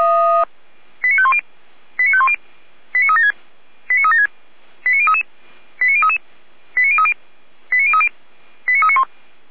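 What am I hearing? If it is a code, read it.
Five-tone selective call (ZVEI), heard as quick runs of five tones, tones of about 70 ms. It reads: 89510 89510 89376 89376 89033 89033 89030 89030 89391
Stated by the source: ZVEI five-tone selective call 89140 received with an SDR by Feuerwehr Orlamünde.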